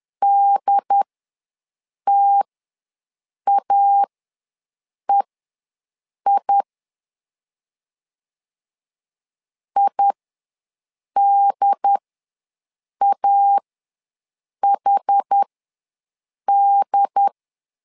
Morse code, heard as a keyed tone, about 10 wpm, spaced as for 6 wpm, 800 Hz. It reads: DTAEI IDAHD
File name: LEZIONI / 2 UHDN / 2 UHD ws 10wpm 2058